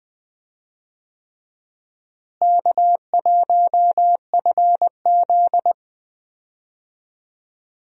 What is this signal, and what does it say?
Morse code, heard as a keyed tone, about 20 wpm, 700 Hz. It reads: K1FZ
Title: USA Call Signs - 4 Characters - Repeated 3x - 20wpm